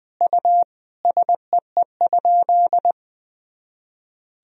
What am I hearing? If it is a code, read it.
Morse code, heard as a keyed tone, about 20 wpm, 700 Hz. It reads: U SEE?
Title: Sentences from Top 100 Words and Abbreviations 20wpm